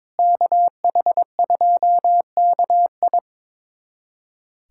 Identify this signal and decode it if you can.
Morse code, heard as a keyed tone, about 22 wpm, 700 Hz. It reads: KH2KI
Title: USA Callsign DE Callsign - 5 Characters - 22wpm